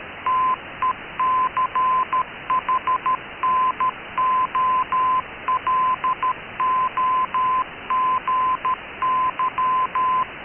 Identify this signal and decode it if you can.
Morse code, heard as a keyed tone, about 13 wpm, 1020 Hz. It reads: TECHNOLOGY
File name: KKS-1 cw 21jul2014 1425z